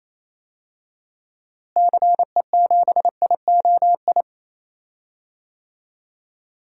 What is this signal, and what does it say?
Morse code, heard as a keyed tone, about 28 wpm, 700 Hz. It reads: CE7IOI
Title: DXCC-100 Call Signs - 6 Characters - Repeated 3x - 28wpm